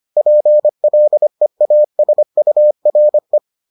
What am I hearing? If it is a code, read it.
Morse code, heard as a keyed tone, about 25 wpm, 600 Hz. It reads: PLEASURE